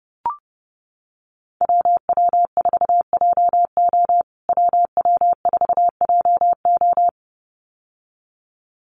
Morse code, heard as a keyed tone, about 30 wpm, 700 Hz. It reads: WW4JO WW4JO
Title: USA Call Signs Repeated 2x 30wpm